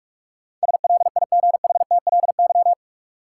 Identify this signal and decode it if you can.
Morse code, heard as a keyed tone, about 46 wpm, 700 Hz. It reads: SLIGHTLY
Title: Sentences from Top 1300 Words - New Word Drill Alt - 45wpm